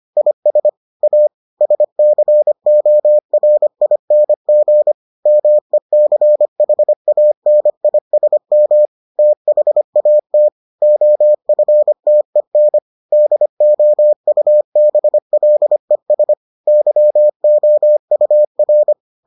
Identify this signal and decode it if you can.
Morse code, heard as a keyed tone, about 25 wpm, 600 Hz. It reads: IS A SCORING MECHANISM THAT OFTEN DOUBLES YOUR